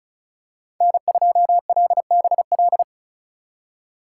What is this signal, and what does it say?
Morse code, heard as a keyed tone, about 36 wpm, 700 Hz. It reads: N2LBL
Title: USA Call Signs - 5 Characters - 2x Character Spacing - 35wpm